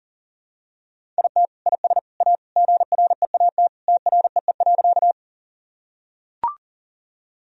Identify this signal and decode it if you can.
Morse code, heard as a keyed tone, about 37 wpm, 700 Hz. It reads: IT IS A GREAT TREE.
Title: Sentences from Top 1000 Words - Review Part 1 - 40wpm